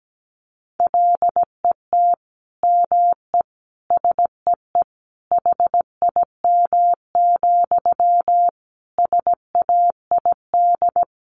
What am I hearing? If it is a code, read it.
Morse code, heard as a keyed tone, about 17 wpm, 700 Hz. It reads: LET ME SEE HIM, SAID
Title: Sentences from Top 400 Words 17wpm